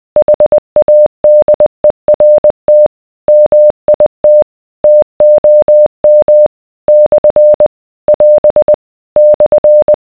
Morse code, heard as a keyed tone, about 20 wpm, 600 Hz. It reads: HADERT MIT TOM / <AS> /